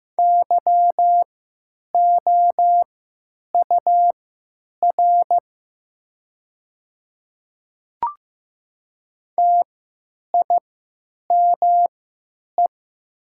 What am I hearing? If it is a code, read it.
Morse code, heard as a keyed tone, about 15 wpm, spaced as for 9 wpm, 700 Hz. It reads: YOUR TIME
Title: 3x Character Spacing - Top 100 Words - 15wpm